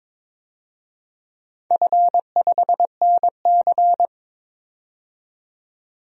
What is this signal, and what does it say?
Morse code, heard as a keyed tone, about 22 wpm, 690 Hz. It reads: F5NC
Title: DXCC-100 Call Signs - 4 Characters - 22wpm